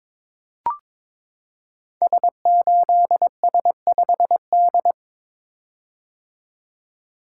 Morse code, heard as a keyed tone, about 22 wpm, 700 Hz. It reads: S8S5D
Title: Code Groups - 5 Letter:Numbers - 22wpm